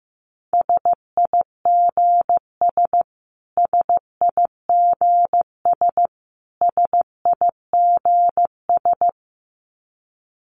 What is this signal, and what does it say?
Morse code, heard as a keyed tone, about 15 wpm, 700 Hz. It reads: SIGS SIGS SIGS
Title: Abbreviations Repeated 3x - 15wpm